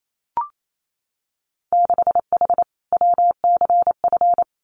Morse code, heard as a keyed tone, about 28 wpm, 700 Hz. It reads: 6H WCF
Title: Winter Field Day Exchanges Encore 28wpm